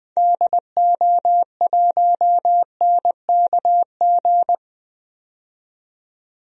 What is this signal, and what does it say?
Morse code, heard as a keyed tone, about 20 wpm, 700 Hz. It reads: DO1NKG
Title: DXCC-100 Call Signs - 6 Characters - 20wpm